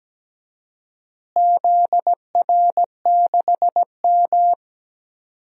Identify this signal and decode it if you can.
Morse code, heard as a keyed tone, about 17 wpm, 700 Hz. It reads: ZR6M
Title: DXCC-100 Call Signs - 4 Characters - Repeated 2x - 17wpm